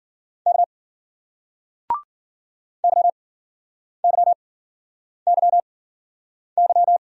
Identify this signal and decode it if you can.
Morse code, heard as a keyed tone, about 54 wpm, spaced as for 9 wpm, 700 Hz. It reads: K YYYY